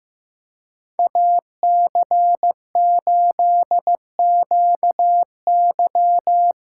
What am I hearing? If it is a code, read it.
Morse code, heard as a keyed tone, about 15 wpm, 700 Hz. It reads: AC8QY
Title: USA Call Signs - 5 Characters - 2x Character Spacing - 15wpm